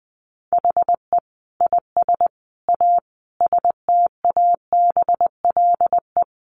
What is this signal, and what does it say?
Morse code, heard as a keyed tone, about 19 wpm, 700 Hz. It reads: HE IS A STABLE
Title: Sentences from Top 1500 Words 20wpm